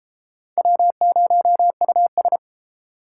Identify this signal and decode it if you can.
Morse code, heard as a keyed tone, about 32 wpm, 700 Hz. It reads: W0US